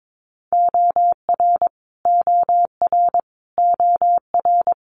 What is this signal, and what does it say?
Morse code, heard as a keyed tone, about 22 wpm, 700 Hz. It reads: OR OR OR
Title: Multi-letter Phonemes Repeated 3x 22wpm